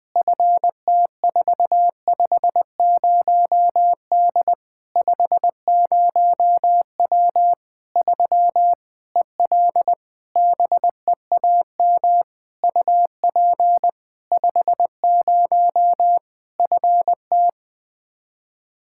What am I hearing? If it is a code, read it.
Morse code, heard as a keyed tone, about 21 wpm, 700 Hz. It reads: FT450D 50W 3 EL BEAM UP 50 FT
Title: K1USN SST QSO Practice 20wpm